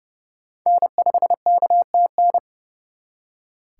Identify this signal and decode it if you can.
Morse code, heard as a keyed tone, about 36 wpm, 700 Hz. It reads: N5KTN
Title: Speed Racing - USA 1x3 Call Signs 30wpm